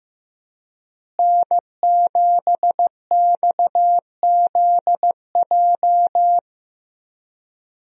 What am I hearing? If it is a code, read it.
Morse code, heard as a keyed tone, about 15 wpm, 700 Hz. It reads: N7XZJ
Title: USA Call Signs - 5 Characters - Repeated 3x - 15wpm